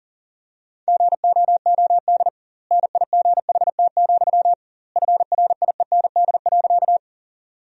Morse code, heard as a keyed tone, about 40 wpm, 700 Hz. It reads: GOOD NIGHT, FRIEND.
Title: 2x Word-Spacing - Sentences from Top 300 Words - 40wpm